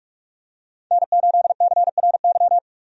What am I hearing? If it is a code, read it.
Morse code, heard as a keyed tone, about 46 wpm, 700 Hz. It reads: N8CRY
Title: USA Call Signs - 5 Characters - 45wpm